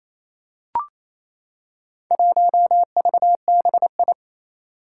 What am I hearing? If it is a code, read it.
Morse code, heard as a keyed tone, about 28 wpm, 700 Hz. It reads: 1VBI
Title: Four Challenging Letter-Number Combinations 28wpm